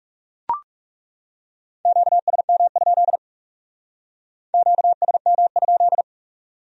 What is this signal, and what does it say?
Morse code, heard as a keyed tone, about 37 wpm, 700 Hz. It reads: QSM? QSM?
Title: Speed-Racing Q-Codes - 30wpm